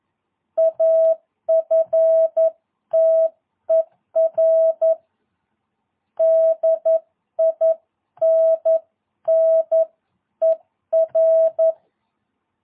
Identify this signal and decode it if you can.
Morse code, heard as a keyed tone, about 11 wpm, 650 Hz. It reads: AFTER DINNER
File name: jh8cbh / 2022cw